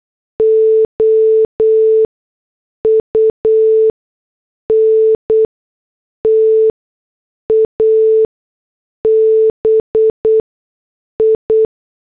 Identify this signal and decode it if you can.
Morse code, heard as a keyed tone, about 8 wpm, spaced as for 6 wpm, 440 Hz. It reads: OUNTABI
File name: morse / headlines